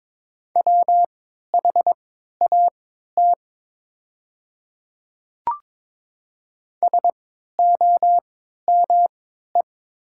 Morse code, heard as a keyed tone, about 23 wpm, spaced as for 13 wpm, 700 Hz. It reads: WHAT SOME
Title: 3x Character Spacing - Top 100 Words - 22wpm